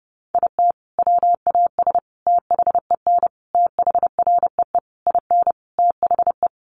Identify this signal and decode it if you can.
Morse code, heard as a keyed tone, about 30 wpm, 700 Hz. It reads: IT WAS THEN THREE IN THE
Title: Sentences from Top 1500 Words 30wpm